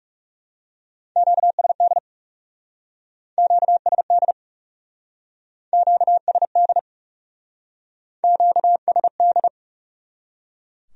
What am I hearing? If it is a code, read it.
Morse code, heard as a keyed tone, about 39 wpm, 700 Hz. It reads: QSD QSD QSD QSD